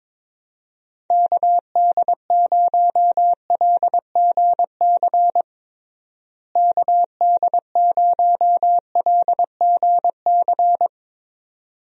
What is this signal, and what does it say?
Morse code, heard as a keyed tone, about 21 wpm, 700 Hz. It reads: KD0LGC KD0LGC
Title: USA Call Signs - 6 Characters - Repeated 2x - 22wpm